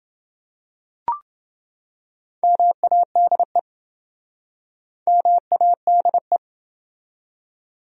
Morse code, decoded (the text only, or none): MADE MADE